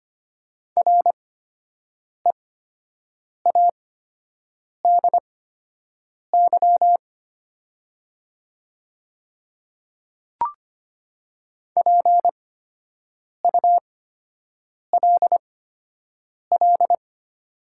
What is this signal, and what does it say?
Morse code, heard as a keyed tone, about 25 wpm, spaced as for 7 wpm, 700 Hz. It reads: READY PULL
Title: 8x Character Spacing - Top 500 Words - 25wpm